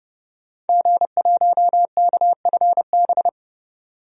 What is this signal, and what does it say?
Morse code, heard as a keyed tone, about 32 wpm, 700 Hz. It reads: G1KFB
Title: DXCC-100 Call Signs - 5 Characters - 30wpm